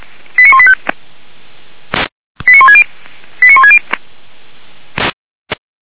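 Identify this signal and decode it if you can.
Five-tone selective call (ZVEI), heard as quick runs of five tones, tones of about 80 ms. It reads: 89185 89160 89160
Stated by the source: Five-tone ZVEI selective call 89180, received on SDR, recorded by feuerwehr-orlamuende (fire brigade Orlamünde).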